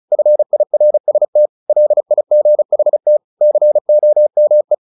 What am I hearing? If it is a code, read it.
Morse code, heard as a keyed tone, about 34 wpm, 600 Hz. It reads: FIRST LIGHT COME
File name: the old man and the sea 2 35wpm